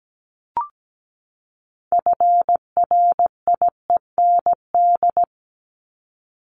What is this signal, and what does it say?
Morse code, heard as a keyed tone, about 17 wpm, 700 Hz. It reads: FRIEND
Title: Top 500 Words 17wpm